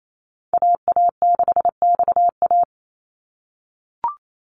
Morse code, heard as a keyed tone, about 28 wpm, 700 Hz. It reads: AA6XA
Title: NA SOTA Activator Callsigns 28wpm